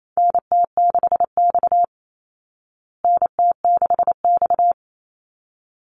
Morse code, heard as a keyed tone, about 28 wpm, 700 Hz. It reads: NT6X NT6X
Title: Call Signs Repeated 3x - 28wpm